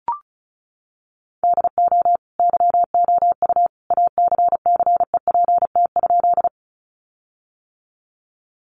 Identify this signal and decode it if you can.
Morse code, heard as a keyed tone, about 34 wpm, 700 Hz. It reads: DO YOU ACCEPT?
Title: Sentences from Top 1300 Words 35wpm